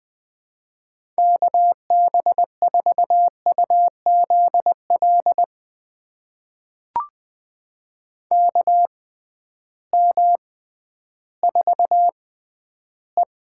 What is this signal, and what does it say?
Morse code, heard as a keyed tone, about 20 wpm, spaced as for 12 wpm, 700 Hz. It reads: KB4UZL KM4E
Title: USA Call Signs - 6x Character Spacing - 20wpm